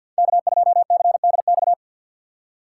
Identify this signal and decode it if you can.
Morse code, heard as a keyed tone, about 52 wpm, 700 Hz. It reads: K2CDX
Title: K1USN SST QSO Practice - 50wpm